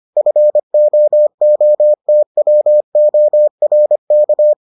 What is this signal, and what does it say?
Morse code, heard as a keyed tone, about 25 wpm, 600 Hz. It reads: FOOTWORK